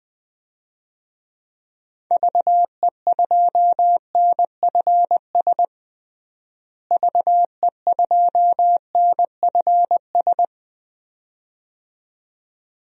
Morse code, heard as a keyed tone, about 20 wpm, 700 Hz. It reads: VE2NFS VE2NFS
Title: DXCC-100 Call Signs - 6 Characters - Repeated 2x - 20wpm